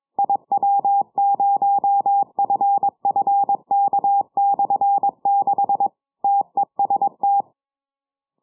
Morse code, heard as a keyed tone, about 22 wpm, 800 Hz. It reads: IW0FFK/6 TEST
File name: iw0ffk / 70cm / 2007 10 07 UR7D IW0FFK 6 980km